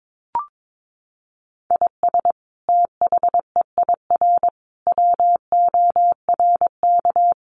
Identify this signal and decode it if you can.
Morse code, heard as a keyed tone, about 22 wpm, 700 Hz. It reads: IS THEIR WORK
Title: Sentences from Top 1500 Words 22wpm